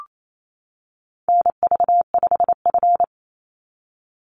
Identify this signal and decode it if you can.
Morse code, heard as a keyed tone, about 28 wpm, 700 Hz. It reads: NV5F
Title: Call Signs - 28wpm